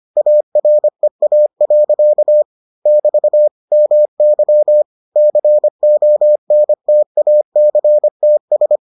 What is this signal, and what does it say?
Morse code, heard as a keyed tone, about 25 wpm, 600 Hz. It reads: AREA. <BT> MY CONTACTS